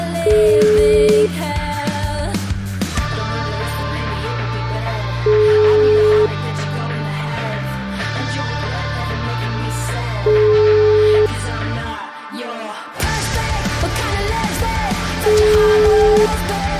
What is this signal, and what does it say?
Ringback tone (Europe, 425 Hz), with noise.